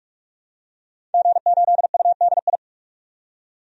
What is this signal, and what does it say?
Morse code, heard as a keyed tone, about 42 wpm, 700 Hz. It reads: G8UDI
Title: DXCC-100 Call Signs - 5 Characters - 45wpm